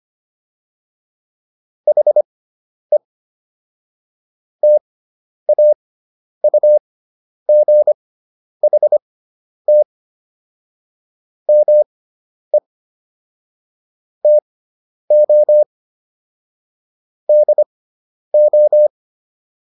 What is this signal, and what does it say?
Morse code, heard as a keyed tone, about 25 wpm, spaced as for 10 wpm, 600 Hz. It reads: HE TAUGHT ME TO DO